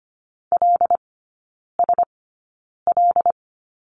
Morse code, heard as a keyed tone, about 25 wpm, spaced as for 9 wpm, 690 Hz. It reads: LSL
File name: Lesson 15e - Number 5 - Three Characters 25wpm